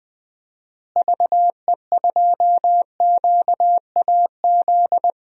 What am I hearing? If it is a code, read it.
Morse code, heard as a keyed tone, about 20 wpm, 700 Hz. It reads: VE2QAZ
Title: DXCC-100 Call Signs - 6 Characters - Repeated 2x - 20wpm